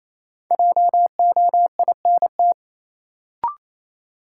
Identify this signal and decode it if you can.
Morse code, heard as a keyed tone, about 27 wpm, 700 Hz. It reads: JOINT
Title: Sentences from Top 1300 Words - New Word Drill Alt - 28wpm